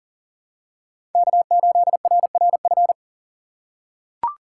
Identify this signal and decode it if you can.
Morse code, heard as a keyed tone, about 40 wpm, 700 Hz. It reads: K8RRF
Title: USA Call Signs - 5 Characters - 6x Character Spacing - 40wpm